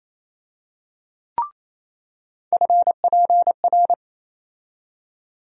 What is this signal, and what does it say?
Morse code, heard as a keyed tone, about 28 wpm, 700 Hz. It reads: FPR
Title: USA Call Sign Postfixes - 3 Letters 28wpm